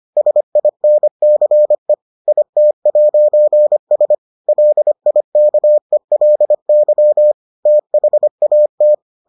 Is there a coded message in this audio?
Yes: Morse code from 0.2 to 8.9 s, heard as a keyed tone, about 25 wpm, 600 Hz. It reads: SINCE IT'S LIKELY THAT